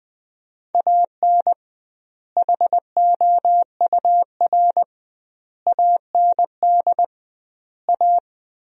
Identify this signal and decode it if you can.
Morse code, heard as a keyed tone, about 20 wpm, 700 Hz. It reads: AN HOUR AND A
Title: 2x Word-Spacing - Sentences from Top 400 Words - 20wpm